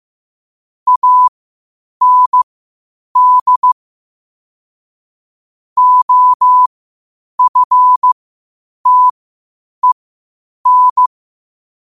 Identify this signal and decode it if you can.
Morse code, heard as a keyed tone, about 15 wpm, spaced as for 8 wpm, 1000 Hz. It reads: AND OFTEN
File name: qotdfiles / 8wpm